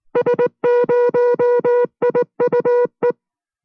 Morse code, heard as a keyed tone, about 19 wpm, 490 Hz. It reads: S0IUE